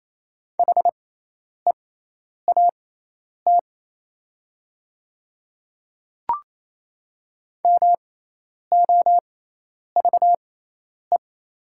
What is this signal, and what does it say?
Morse code, heard as a keyed tone, about 28 wpm, spaced as for 10 wpm, 700 Hz. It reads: HEAT MOVE
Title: 6x Character Spacing - Top 500 Words - 28wpm